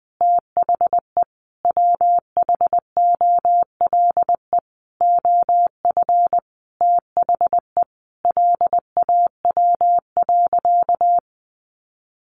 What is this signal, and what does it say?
Morse code, heard as a keyed tone, about 20 wpm, 700 Hz. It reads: THE WHOLE OF THE LAW.
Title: Sentences from Top 1400 Words 20wpm